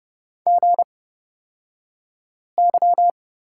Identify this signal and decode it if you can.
Morse code, heard as a keyed tone, about 30 wpm, spaced as for 5 wpm, 690 Hz. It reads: GY
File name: Lesson 36d - BT - Two Characters 30wpm